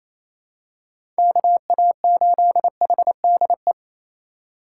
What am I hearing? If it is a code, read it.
Morse code, heard as a keyed tone, about 28 wpm, 700 Hz. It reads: KA8HDE
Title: K1USN SST QSO Practice - 28wpm